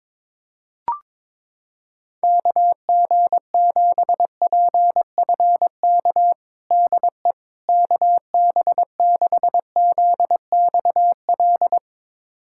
Morse code, heard as a keyed tone, about 22 wpm, 700 Hz. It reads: KG7PFK DE KB6ZXL
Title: USA Callsign DE Callsign - 6 Characters - 22wpm